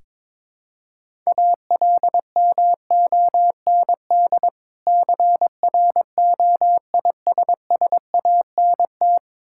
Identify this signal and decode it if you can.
Morse code, heard as a keyed tone, about 22 wpm, 690 Hz. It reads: ALMOND CROISSANT
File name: Difficult Words with Hints - 22wpm